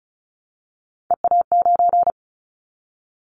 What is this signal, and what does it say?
Morse code, heard as a keyed tone, about 34 wpm, 700 Hz. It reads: EA9
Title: Callsign Prefixes for Top 100 DXCC - 35wpm